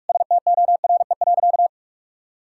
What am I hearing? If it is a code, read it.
Morse code, heard as a keyed tone, about 46 wpm, 700 Hz. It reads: STORE.
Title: Sentences from Top 1000 Words - Review Part 5 - 45wpm